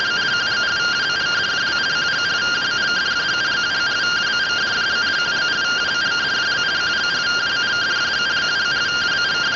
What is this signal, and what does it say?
Two-tone teleprinter signal, about 170 Hz shift.